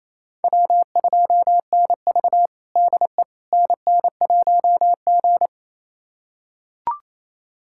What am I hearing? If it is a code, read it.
Morse code, heard as a keyed tone, about 28 wpm, 700 Hz. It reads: W2NV DE NN1G
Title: USA Callsign DE Callsign - 4 Characters - 28wpm